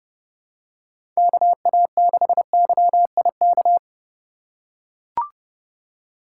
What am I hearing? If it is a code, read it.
Morse code, heard as a keyed tone, about 30 wpm, 700 Hz. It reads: KA6YIK